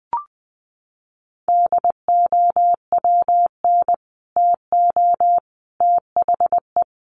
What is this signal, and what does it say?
Morse code, heard as a keyed tone, about 20 wpm, 700 Hz. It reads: DOWN TO THE
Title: Sets of 3 Words 20wpm